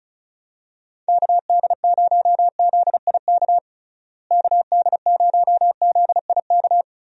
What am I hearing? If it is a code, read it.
Morse code, heard as a keyed tone, about 36 wpm, 700 Hz. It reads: KD0ZIK KD0ZIK
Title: USA Call Signs - 6 Characters - Repeated 3x - 35wpm